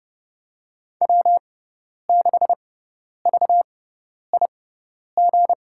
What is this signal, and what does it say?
Morse code, heard as a keyed tone, about 32 wpm, spaced as for 10 wpm, 700 Hz. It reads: W6VIG
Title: USA Call Signs - 5 Characters - 6x Character Spacing - 30wpm